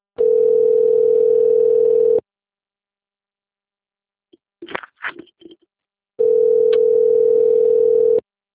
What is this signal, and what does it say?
Ringback tone (North America), 440 + 480 Hz.